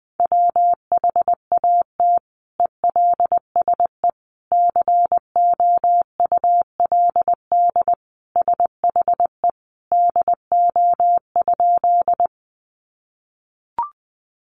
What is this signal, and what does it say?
Morse code, heard as a keyed tone, about 20 wpm, 700 Hz. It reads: WHAT ELSE COULD SHE DO?
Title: Sentences from Top 800 Words 20wpm